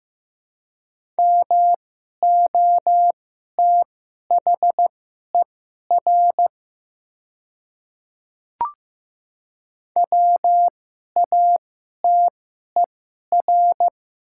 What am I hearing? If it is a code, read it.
Morse code, heard as a keyed tone, about 15 wpm, spaced as for 11 wpm, 700 Hz. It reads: MOTHER WATER
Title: 2x Character Spacing - Top 200 Words - 15wpm